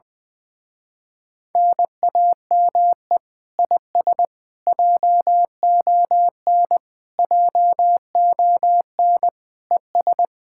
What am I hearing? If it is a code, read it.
Morse code, heard as a keyed tone, about 20 wpm, 700 Hz. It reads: NAME IS JON JON ES
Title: Speed-Racing Common QSO Phrases - 20wpm